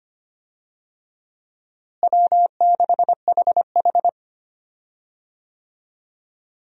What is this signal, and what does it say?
Morse code, heard as a keyed tone, about 25 wpm, 700 Hz. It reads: W6HH
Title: USA Call Signs - 4 Characters - Repeated 2x - 25wpm